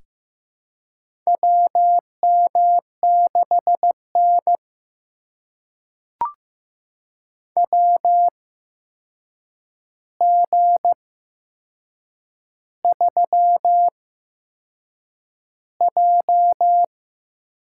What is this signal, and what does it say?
Morse code, heard as a keyed tone, about 15 wpm, spaced as for 6 wpm, 700 Hz. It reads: WM6N WG3J
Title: USA Call Signs - 8x Character Spacing - 15wpm